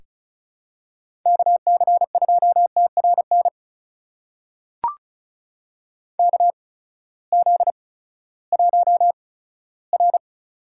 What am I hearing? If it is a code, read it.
Morse code, heard as a keyed tone, about 36 wpm, spaced as for 18 wpm, 700 Hz. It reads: KC2TRN KZ1R
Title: USA Call Signs - 8x Character Spacing - 35wpm